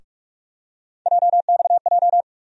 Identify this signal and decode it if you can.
Morse code, heard as a keyed tone, about 44 wpm, 700 Hz. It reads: JXJ
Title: USA Call Sign Postfixes - 3 Letters 45wpm